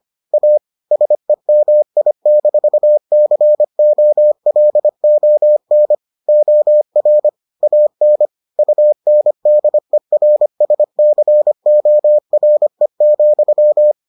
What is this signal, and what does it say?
Morse code, heard as a keyed tone, about 25 wpm, 600 Hz. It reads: A SEMI-COLON OR AN UNDERSCORE,